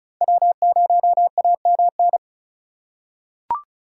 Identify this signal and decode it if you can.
Morse code, heard as a keyed tone, about 36 wpm, 700 Hz. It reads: W0AMN
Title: USA Call Signs - 4x Character Spacing - 35wpm